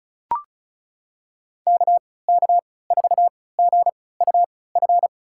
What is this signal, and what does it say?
Morse code, heard as a keyed tone, about 36 wpm, spaced as for 20 wpm, 700 Hz. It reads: KK4GUF
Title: USA Call Signs - 6 Characters - 3x Character Spacing - 35wpm